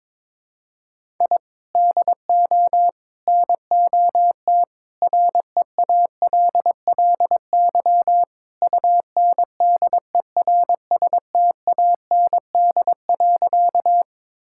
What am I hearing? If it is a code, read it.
Morse code, heard as a keyed tone, about 22 wpm, 700 Hz. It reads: I DO NOT REALLY UNDERSTAND.
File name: practice / 22wpm / Sentences with 4-Word Phrases - 22wpm